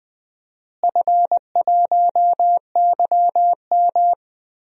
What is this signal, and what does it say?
Morse code, heard as a keyed tone, about 20 wpm, 700 Hz. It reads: F1YM
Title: DXCC-100 Call Signs - 4 Characters - Repeated 2x - 20wpm